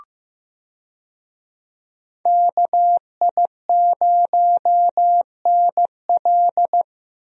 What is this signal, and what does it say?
Morse code, heard as a keyed tone, about 15 wpm, 700 Hz. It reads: KI0NL
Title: USA Call Signs - 5 Characters - Repeated 3x - 15wpm